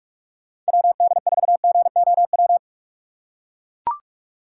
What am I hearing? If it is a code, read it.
Morse code, heard as a keyed tone, about 42 wpm, 700 Hz. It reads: WD4GOW
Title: USA Call Signs - 6 Characters - 2x Character Spacing - 45wpm